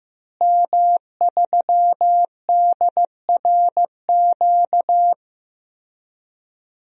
Morse code, heard as a keyed tone, about 15 wpm, 700 Hz. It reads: M3DRQ
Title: DXCC-100 Call Signs - 5 Characters - Repeated 2x - 15wpm